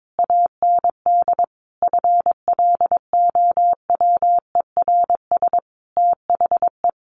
Morse code, heard as a keyed tone, about 22 wpm, 700 Hz. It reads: AND FLOWERS THE
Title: A Sound Of Thunder 22wpm